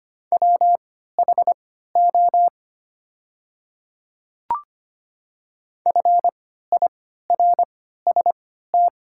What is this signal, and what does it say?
Morse code, heard as a keyed tone, about 25 wpm, spaced as for 14 wpm, 700 Hz. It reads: WHO FIRST
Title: 3x Character Spacing - Top 100 Words - 25wpm